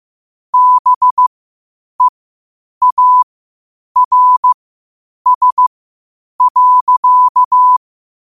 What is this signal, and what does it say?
Morse code, heard as a keyed tone, about 15 wpm, spaced as for 8 wpm, 1000 Hz. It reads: BEARS.
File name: qotdfiles / 8wpm